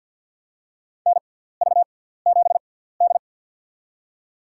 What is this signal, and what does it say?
Morse code, heard as a keyed tone, about 53 wpm, spaced as for 17 wpm, 690 Hz. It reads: NV7D
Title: USA Call Signs - 4 Characters - 6x Character Spacing - 50wpm